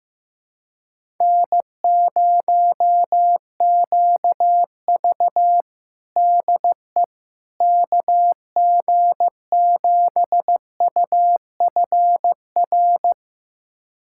Morse code, heard as a keyed tone, about 16 wpm, 700 Hz. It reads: N0QV DE KG7UFR
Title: USA Callsign DE Callsign 15wpm